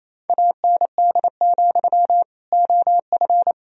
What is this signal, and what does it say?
Morse code, heard as a keyed tone, about 28 wpm, 700 Hz. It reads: AND, OF
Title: Sentences from Top 1000 Words - Review Part 1 - 28wpm